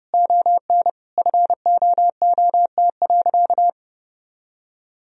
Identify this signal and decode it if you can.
Morse code, heard as a keyed tone, about 30 wpm, 700 Hz. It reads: ON FOOT.
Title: Sentences from Top 1000 Words - Review Part 3 - 30wpm